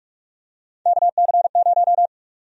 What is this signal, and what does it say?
Morse code, heard as a keyed tone, about 47 wpm, 700 Hz. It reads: KC0
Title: USA Call Sign Prefixes - 45wpm